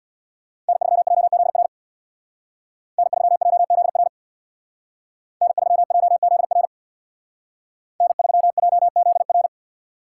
Morse code, heard as a keyed tone, about 59 wpm, 700 Hz. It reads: N3JZR N3JZR N3JZR N3JZR
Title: Speed Racing - USA 1x3 Call Signs 50wpm